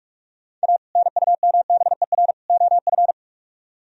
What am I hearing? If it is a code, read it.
Morse code, heard as a keyed tone, about 44 wpm, 700 Hz. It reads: A NUMBER OF